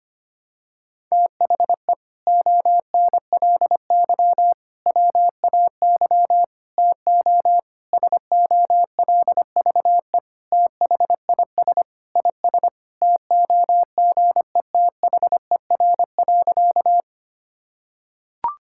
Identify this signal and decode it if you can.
Morse code, heard as a keyed tone, about 25 wpm, 700 Hz. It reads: THE ONLY WAY TO SOLVE THIS IS TOGETHER.
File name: practice / 25wpm / Sentences with 4-Word Phrases - 25wpm